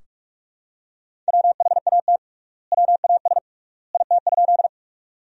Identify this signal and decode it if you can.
Morse code, heard as a keyed tone, about 46 wpm, 700 Hz. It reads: WHAT WAS IT?
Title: 3x Word-Spacing - Sentences from Top 100 Words - 45wpm